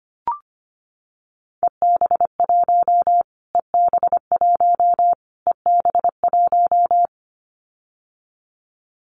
Morse code, heard as a keyed tone, about 25 wpm, 700 Hz. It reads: EB1 EB1 EB1
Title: Callsign Prefixes for Top 25 DXCC - Repeated 3x - 25wpm